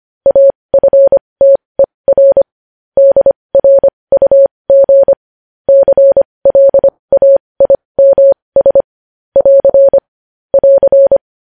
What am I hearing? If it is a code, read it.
Morse code, heard as a keyed tone, about 25 wpm, spaced as for 20 wpm, 550 Hz. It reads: AFTER DRUG CLAIMS <AR> <AR>
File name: podcast / News headlines-1x-25x20